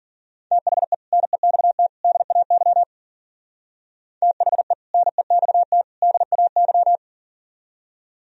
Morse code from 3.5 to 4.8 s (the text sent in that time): THE